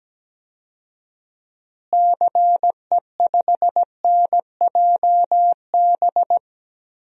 Morse code, heard as a keyed tone, about 17 wpm, 700 Hz. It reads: CE5NJB
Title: DXCC-100 Call Signs - 6 Characters - Repeated 3x - 17wpm